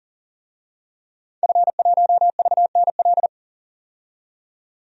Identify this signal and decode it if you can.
Morse code, heard as a keyed tone, about 40 wpm, 700 Hz. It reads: F1VNL
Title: DXCC-100 Call Signs - 5 Characters - 40wpm